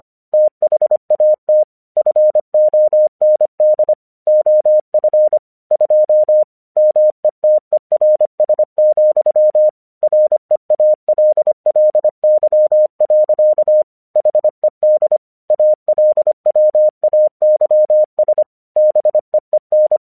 Morse code, heard as a keyed tone, about 25 wpm, 610 Hz. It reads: THAT FOND OF 2 METERS, REALLY. HED ALWAYS BEEN
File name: 1 English 25wpm / AQC08